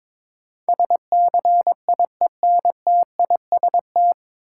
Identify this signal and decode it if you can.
Morse code, heard as a keyed tone, about 21 wpm, 700 Hz. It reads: SCIENTIST
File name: 9 Letter Words - First 3 Letters Given - 22wpm